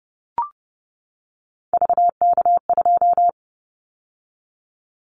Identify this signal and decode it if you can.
Morse code, heard as a keyed tone, about 30 wpm, 700 Hz. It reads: VK2
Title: Callsign Prefixes for Top 100 DXCC - 30wpm